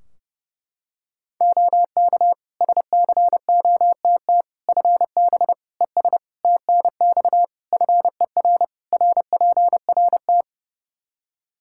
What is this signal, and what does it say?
Morse code, heard as a keyed tone, about 30 wpm, 700 Hz. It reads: OK SCOTT FB ES TNX FER RPRT
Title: Practice QSO - Exchange 2 - Info - 30wpm